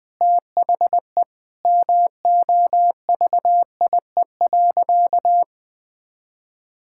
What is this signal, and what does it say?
Morse code, heard as a keyed tone, about 20 wpm, 700 Hz. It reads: THE MOVIE.
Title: Sentences from Top 1000 Words - Review Part 5 - 20wpm